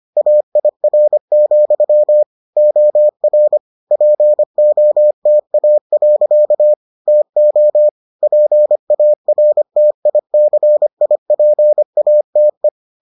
Morse code, heard as a keyed tone, about 25 wpm, 600 Hz. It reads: AIR, OR POTA. TO PARTICIPATE